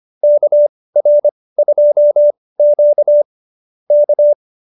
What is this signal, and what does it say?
Morse code, heard as a keyed tone, about 25 wpm, spaced as for 18 wpm, 590 Hz. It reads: KR2Q K